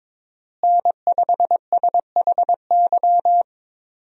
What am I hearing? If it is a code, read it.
Morse code, heard as a keyed tone, about 22 wpm, 700 Hz. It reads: N5SHY
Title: USA Call Signs - 5 Characters - Repeated 2x - 22wpm